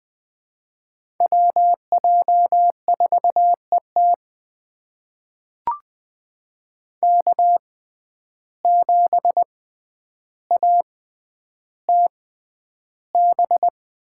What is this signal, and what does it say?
Morse code, heard as a keyed tone, about 20 wpm, spaced as for 10 wpm, 700 Hz. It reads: WJ4ET K7ATB
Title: USA Call Signs - 5 Characters - 6x Character Spacing - 20wpm